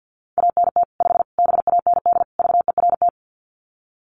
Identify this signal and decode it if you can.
Morse code, heard as a keyed tone, about 50 wpm, 700 Hz. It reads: ANT 5 BAND VERT
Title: Common QSO Phrases 50wpm